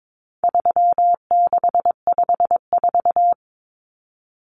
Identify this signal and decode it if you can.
Morse code, heard as a keyed tone, about 22 wpm, 700 Hz. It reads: 3654
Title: Common Cut Numbers - 4-Digit 22wpm